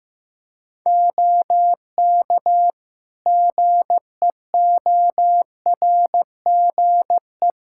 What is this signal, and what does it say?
Morse code, heard as a keyed tone, about 15 wpm, 700 Hz. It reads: OK GEORGE
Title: K1USN SST QSO Practice 15wpm